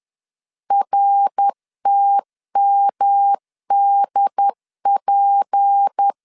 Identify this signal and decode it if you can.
Morse code, heard as a keyed tone, about 10 wpm, 800 Hz. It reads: RTMDP
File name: CW / DET / 1 6 1500C S B 10wpm 4137